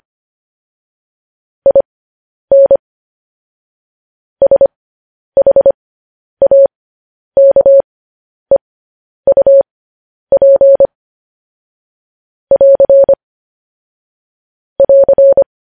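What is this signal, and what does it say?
Morse code, heard as a keyed tone, about 25 wpm, spaced as for 10 wpm, 550 Hz. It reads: IN SHAKEUP <AR> <AR>
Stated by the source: Morse news headlines sent at 10 wpm.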